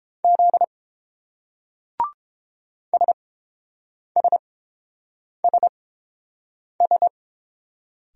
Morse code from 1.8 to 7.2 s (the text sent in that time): SSSS